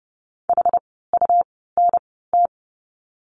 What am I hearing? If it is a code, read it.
Morse code, heard as a keyed tone, about 30 wpm, spaced as for 17 wpm, 700 Hz. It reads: HUNT
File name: Lesson 17f - Letter u - Words 30wpm